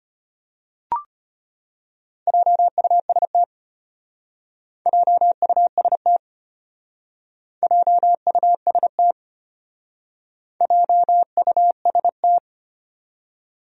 Morse code, heard as a keyed tone, about 31 wpm, 700 Hz. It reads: JUST JUST JUST JUST